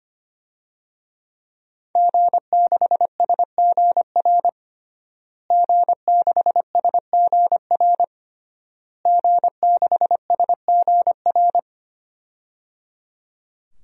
Morse code, heard as a keyed tone, about 25 wpm, 700 Hz. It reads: G6SGR G6SGR G6SGR
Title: DXCC-100 Call Signs - 5 Characters - Repeated 3x - 25wpm